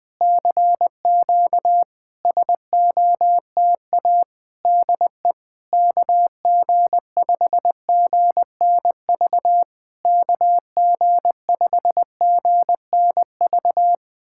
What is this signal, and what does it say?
Morse code, heard as a keyed tone, about 20 wpm, 700 Hz. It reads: CQ SOTA DE KG5GNV KG5GNV
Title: SOTA QSO Practice 20wpm